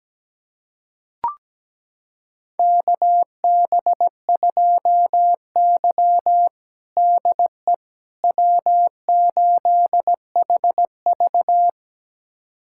Morse code, heard as a keyed tone, about 17 wpm, 700 Hz. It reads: KB2Y DE W8HV
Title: USA Callsign DE Callsign - 4 Characters - 17wpm